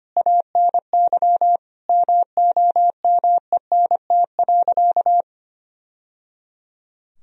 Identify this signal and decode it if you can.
Morse code, heard as a keyed tone, about 25 wpm, 700 Hz. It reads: ANY MOMENT.
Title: Sentences from Top 1000 Words - Review Part 3 - 25wpm